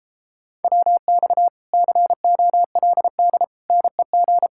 Morse code, heard as a keyed tone, about 29 wpm, 700 Hz. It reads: WX COLD NEG